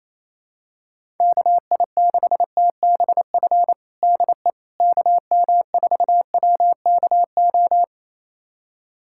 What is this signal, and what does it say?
Morse code, heard as a keyed tone, about 28 wpm, 700 Hz. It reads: KI6TBF DE KM4WKO
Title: USA Callsign DE Callsign 28wpm